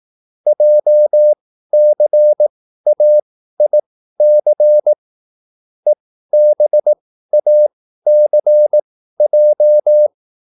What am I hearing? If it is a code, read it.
Morse code, heard as a keyed tone, about 18 wpm, spaced as for 13 wpm, 600 Hz. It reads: JCAIC EBACJ
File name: QSL / CW lesson